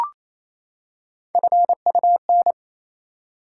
Morse code, heard as a keyed tone, about 28 wpm, 700 Hz. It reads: FUN
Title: Three Letter Words - 28wpm